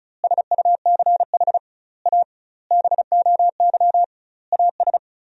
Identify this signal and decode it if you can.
Morse code, heard as a keyed tone, about 36 wpm, 700 Hz. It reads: SUCH A BOY AS
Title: 2x Word-Spacing - Sentences from Top 400 Words - 35wpm